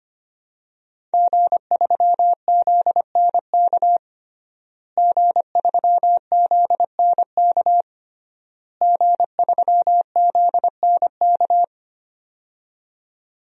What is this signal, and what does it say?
Morse code, heard as a keyed tone, about 25 wpm, 700 Hz. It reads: G3ZNK G3ZNK G3ZNK
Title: DXCC-100 Call Signs - 5 Characters - Repeated 3x - 25wpm